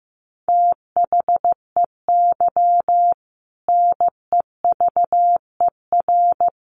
Morse code, heard as a keyed tone, about 15 wpm, 700 Hz. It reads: THEY NEVER
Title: Sentences from Top 1200 Words 15wpm